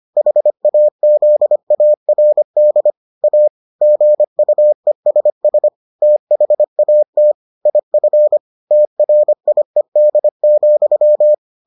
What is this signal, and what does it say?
Morse code, heard as a keyed tone, about 25 wpm, 600 Hz. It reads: HAZARD A GUESS THAT IF TRIED,